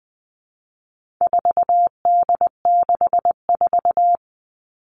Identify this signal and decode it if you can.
Morse code, heard as a keyed tone, about 20 wpm, 700 Hz. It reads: 4D64
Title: All Cut Numbers - 4-Digit 20wpm